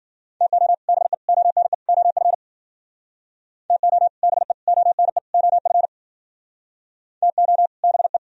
Morse code, heard as a keyed tone, about 54 wpm, 700 Hz. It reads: TO BE ONE OF TO BE ONE OF TO BE